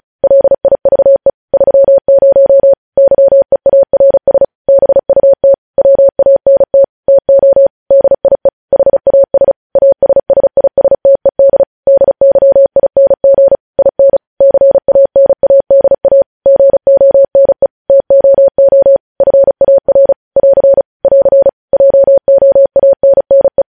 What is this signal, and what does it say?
Morse code, heard as a keyed tone, about 36 wpm, 550 Hz. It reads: LIVE 30 YEARS BUT WANT TO DIE HAS ASSISTED DYING IN CANADA GONE TOO FAR <AR> <AR> JOANNE
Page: news headlines in morse code at 35 wpm